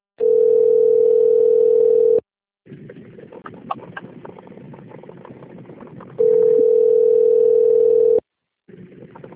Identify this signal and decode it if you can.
Ringback tone (North America), 440 + 480 Hz.